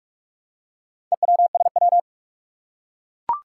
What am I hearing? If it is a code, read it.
Morse code, heard as a keyed tone, about 44 wpm, 700 Hz. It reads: EWSW
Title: Four Challenging Letter-Number Combinations 45wpm